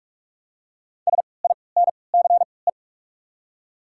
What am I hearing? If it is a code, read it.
Morse code, heard as a keyed tone, about 43 wpm, spaced as for 25 wpm, 700 Hz. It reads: SINCE
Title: 3x Character Spacing - Top 500 Words - 45wpm